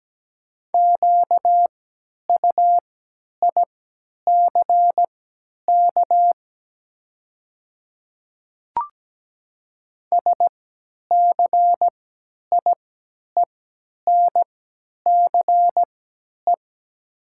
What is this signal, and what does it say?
Morse code, heard as a keyed tone, about 17 wpm, spaced as for 10 wpm, 700 Hz. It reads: QUICK SCIENCE
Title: 3x Character Spacing - Top 500 Words - 17wpm